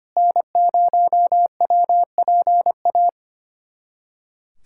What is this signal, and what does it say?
Morse code, heard as a keyed tone, about 25 wpm, 700 Hz. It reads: N0WPA
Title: USA Callsign DE Callsign 25wpm